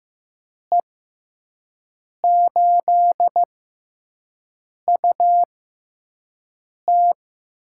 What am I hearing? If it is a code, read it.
Morse code, heard as a keyed tone, about 15 wpm, spaced as for 5 wpm, 700 Hz. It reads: E8UT
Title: USA Call Signs - 5 Characters - 6x Character Spacing - 15wpm